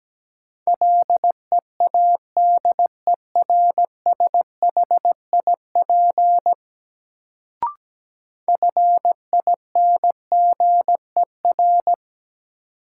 Morse code, heard as a keyed tone, about 17 wpm, 700 Hz. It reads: LEADERSHIP FINGER
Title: Sentences from Top 1600 Words - New Word Drill Alt - 17wpm